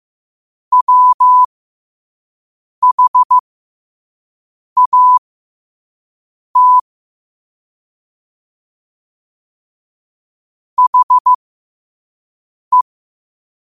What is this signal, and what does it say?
Morse code, heard as a keyed tone, about 15 wpm, spaced as for 5 wpm, 1000 Hz. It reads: WHAT HE
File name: qotdfiles / 5wpm